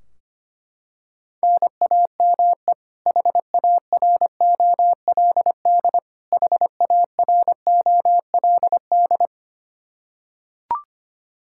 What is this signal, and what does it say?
Morse code, heard as a keyed tone, about 25 wpm, 700 Hz. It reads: NAME HAROLD HAROLD
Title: K1USN SST QSO Practice 25wpm